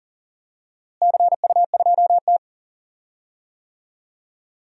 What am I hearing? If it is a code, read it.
Morse code, heard as a keyed tone, about 40 wpm, 700 Hz. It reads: CU2T
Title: DXCC-100 Call Signs - 4 Characters - Repeated 3x - 40wpm